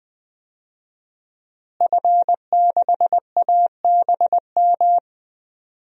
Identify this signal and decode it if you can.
Morse code, heard as a keyed tone, about 20 wpm, 700 Hz. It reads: F6ABM
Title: DXCC-100 Call Signs - 5 Characters - Repeated 3x - 20wpm